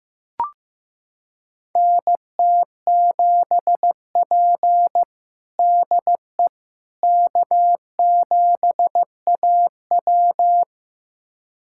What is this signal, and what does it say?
Morse code, heard as a keyed tone, about 15 wpm, 700 Hz. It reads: NT7P DE K7AW
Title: USA Callsign DE Callsign - 4 Characters - 15wpm